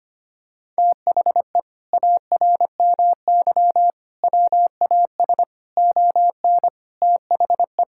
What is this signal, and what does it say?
Morse code, heard as a keyed tone, about 25 wpm, 700 Hz. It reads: THE ARMY WAS ON THE
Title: Sentences from Top 1000 Words - Review Part 5 - 25wpm